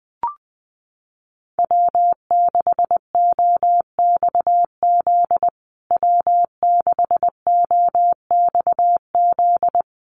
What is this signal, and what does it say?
Morse code, heard as a keyed tone, about 20 wpm, 700 Hz. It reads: W6OXZ W6OXZ
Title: USA Call Signs Repeated 2x 20wpm